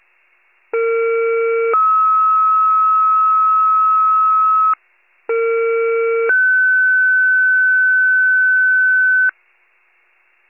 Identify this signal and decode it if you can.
Two-tone page, heard as a one-second tone, then a three-second tone, about 460 then 1290 Hz, about 460 then 1500 Hz. It is heard from 0.7 to 9.3 s.
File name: pager / 21 FIRE / wcalert